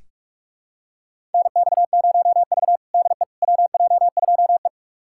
Morse code, heard as a keyed tone, about 43 wpm, 700 Hz. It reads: NX0V DE WJ2E
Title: USA Callsign DE Callsign - 4 Characters - 45wpm